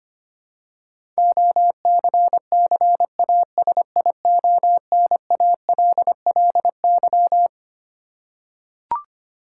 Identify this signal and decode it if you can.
Morse code, heard as a keyed tone, about 25 wpm, 700 Hz. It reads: OCCASIONALLY
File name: Twelve Letter Words - 25wpm